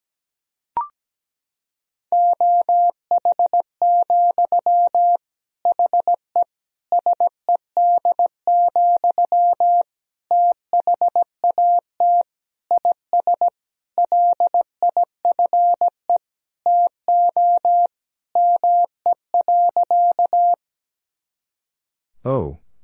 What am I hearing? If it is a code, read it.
Morse code, heard as a keyed tone, about 17 wpm, 700 Hz. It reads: OH, HE SED, THAT IS LIFE TO ME.
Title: Sentences from Top 500 Words and Abbreviations 17wpm